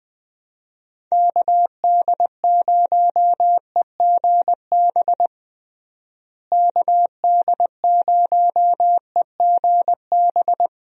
Morse code, heard as a keyed tone, about 20 wpm, 700 Hz. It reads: KD0EGB KD0EGB
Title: USA Call Signs - 6 Characters - Repeated 2x - 20wpm